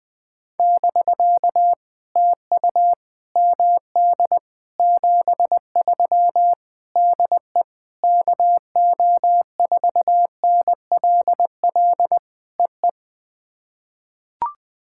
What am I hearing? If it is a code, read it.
Morse code, heard as a keyed tone, about 21 wpm, 700 Hz. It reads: <BK> TU MD 73 DE KO4NLL EE
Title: POTA QSO Practice - 20wpm